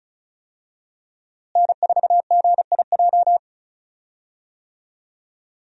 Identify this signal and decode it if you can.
Morse code, heard as a keyed tone, about 36 wpm, 700 Hz. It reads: N4GIJ
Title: USA Call Signs - 5 Characters - Repeated 2x - 35wpm